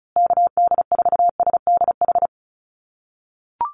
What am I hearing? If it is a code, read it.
Morse code, heard as a keyed tone, about 34 wpm, 700 Hz. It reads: KD4SDH
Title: USA Call Signs Repeated 2x 35wpm